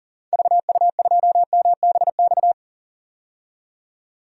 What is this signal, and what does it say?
Morse code, heard as a keyed tone, about 40 wpm, 700 Hz. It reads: VU2MBX
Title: DXCC-100 Call Signs - 6 Characters - 40wpm